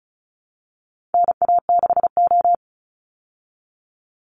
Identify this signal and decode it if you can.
Morse code, heard as a keyed tone, about 34 wpm, 700 Hz. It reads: NA6O
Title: Call Signs Repeated 2x - 35wpm